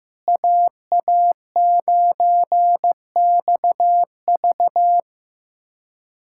Morse code, heard as a keyed tone, about 15 wpm, 700 Hz. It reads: AA9XV
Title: USA Callsign DE Callsign - 5 Characters - 15wpm